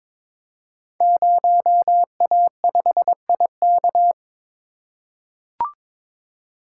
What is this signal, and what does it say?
Morse code, heard as a keyed tone, about 23 wpm, 700 Hz. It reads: 0A5IK
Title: Code Groups - 5 Letter:Numbers - 22wpm